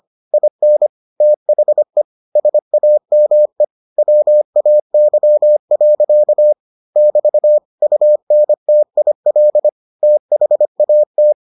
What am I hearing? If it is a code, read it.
Morse code, heard as a keyed tone, about 25 wpm, 600 Hz. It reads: IN THE SAME WAY. <BT> UNTIL THAT